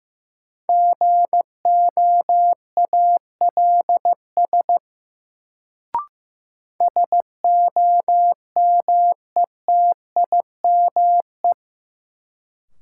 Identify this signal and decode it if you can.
Morse code, heard as a keyed tone, about 15 wpm, 700 Hz. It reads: GOALS SOMETIME
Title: Sentences from Top 1000 Words - Review Part 6 - New Word Drill - 15wpm